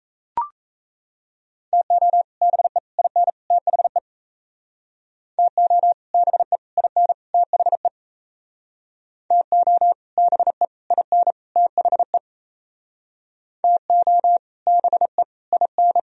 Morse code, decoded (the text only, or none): TO BE IN THE TO BE IN THE TO BE IN THE TO BE IN